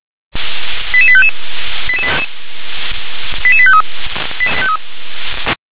Five-tone selective call (ZVEI), heard as quick runs of five tones, tones of about 70 ms. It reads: 88855 88853